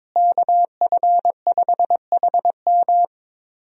Morse code, heard as a keyed tone, about 22 wpm, 700 Hz. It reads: KF5HM
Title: USA Call Signs - 5 Characters - Repeated 3x - 22wpm